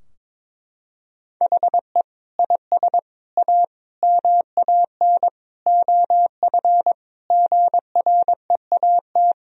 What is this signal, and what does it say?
Morse code, heard as a keyed tone, about 22 wpm, 700 Hz. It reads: HE IS A MAN OF GREAT
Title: Sentences from Top 1000 Words - Review Part 3 - 22wpm